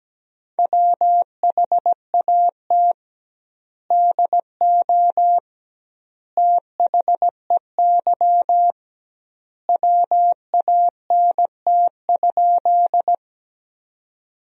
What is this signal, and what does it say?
Morse code, heard as a keyed tone, about 17 wpm, 700 Hz. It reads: WHAT DO THEY WANT?
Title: 2x Word-Spacing - Sentences from Top 200 Words - 17wpm